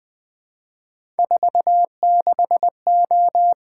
Morse code, heard as a keyed tone, about 20 wpm, 700 Hz. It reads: 46O
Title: Three Letter-Number Combinations - 20wpm